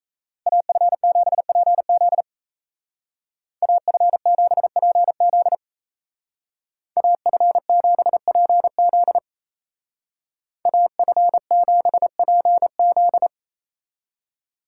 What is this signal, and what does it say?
Morse code, heard as a keyed tone, about 34 wpm, 700 Hz. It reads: AF7PZ AF7PZ AF7PZ AF7PZ
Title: Speed Racing - USA 2x2 Call Signs 28wpm